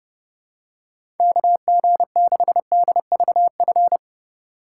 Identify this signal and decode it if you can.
Morse code, heard as a keyed tone, about 30 wpm, 700 Hz. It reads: KG6DVF
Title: USA Call Signs - 6 Characters - 2x Character Spacing - 30wpm